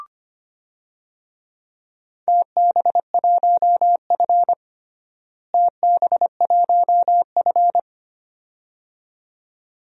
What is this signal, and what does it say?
Morse code, heard as a keyed tone, about 25 wpm, 700 Hz. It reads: TB1F TB1F
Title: DXCC-100 Call Signs - 4 Characters - Repeated 2x - 25wpm